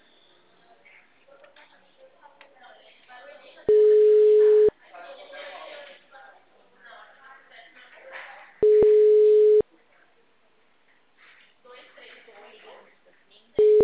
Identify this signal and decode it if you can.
Ringback tone (Europe, 425 Hz), with speech.